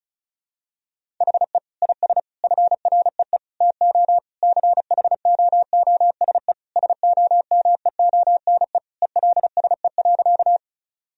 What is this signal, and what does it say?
Morse code, heard as a keyed tone, about 34 wpm, 700 Hz. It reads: HE IS FREE TO CHOOSE SOMEONE ELSE.
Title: Sentences from Top 1000 Words - Review Part 4 - 35wpm